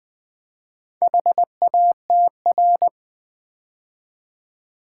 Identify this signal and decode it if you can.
Morse code, heard as a keyed tone, about 20 wpm, 700 Hz. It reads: HATR
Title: Build A Word - 2 Letters At A Time - 6-Letter Words - 20wpm